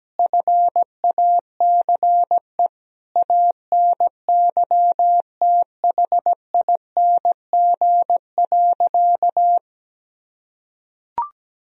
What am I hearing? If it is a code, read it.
Morse code, heard as a keyed tone, about 17 wpm, 700 Hz. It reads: FACE ANYTHING.
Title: Sentences from Top 1000 Words - Review Part 2 - 17wpm